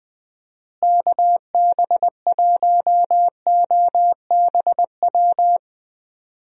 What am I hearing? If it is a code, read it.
Morse code, heard as a keyed tone, about 20 wpm, 700 Hz. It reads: KB1OBW
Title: USA Call Signs - 6 Characters - Repeated 3x - 20wpm